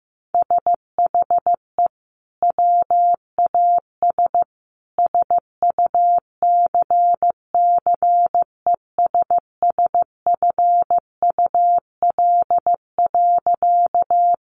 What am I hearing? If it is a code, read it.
Morse code, heard as a keyed tone, about 15 wpm, 700 Hz. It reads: SHE WAS SUCCESSFUL.
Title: Sentences from Top 900 Words 15wpm